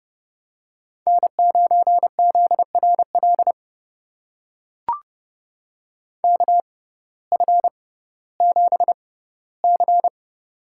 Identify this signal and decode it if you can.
Morse code, heard as a keyed tone, about 30 wpm, spaced as for 17 wpm, 700 Hz. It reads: N9ZRL KF7C
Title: USA Call Signs - 5 Characters - 6x Character Spacing - 30wpm